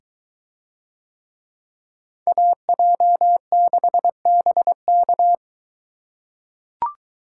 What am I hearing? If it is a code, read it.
Morse code, heard as a keyed tone, about 22 wpm, 700 Hz. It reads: AJ6BK